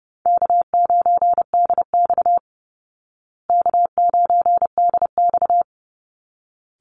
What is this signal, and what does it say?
Morse code, heard as a keyed tone, about 30 wpm, 700 Hz. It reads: K9DX K9DX
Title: Call Signs Repeated 3x - 30wpm